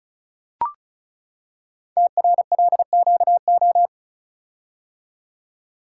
Morse code, heard as a keyed tone, about 36 wpm, 700 Hz. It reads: TRLQO
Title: Code Groups - 5 Letter:Numbers - 35wpm